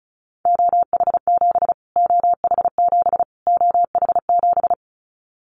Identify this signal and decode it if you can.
Morse code, heard as a keyed tone, about 34 wpm, 700 Hz. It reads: OH7 OH7 OH7
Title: Callsign Prefixes for Top 25 DXCC - Repeated 3x - 35wpm